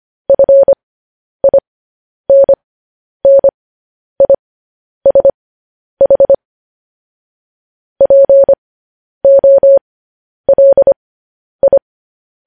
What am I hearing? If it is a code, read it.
Morse code, heard as a keyed tone, about 25 wpm, spaced as for 10 wpm, 550 Hz. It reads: FINNISH POLI
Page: News Headlines in Morse Code at 10 WPM